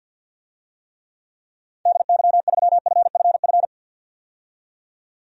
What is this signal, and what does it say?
Morse code, heard as a keyed tone, about 53 wpm, 700 Hz. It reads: DY3FFF
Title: DXCC-100 Call Signs - 6 Characters - 50wpm